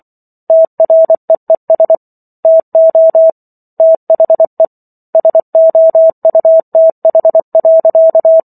Morse code, heard as a keyed tone, about 24 wpm, 650 Hz. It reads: TREES TO THE SOUTH.